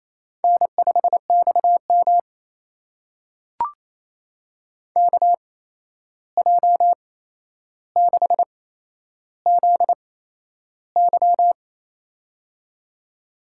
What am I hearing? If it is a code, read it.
Morse code, heard as a keyed tone, about 28 wpm, spaced as for 11 wpm, 700 Hz. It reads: N5XM KJ6ZY
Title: USA Call Signs - 8x Character Spacing - 28wpm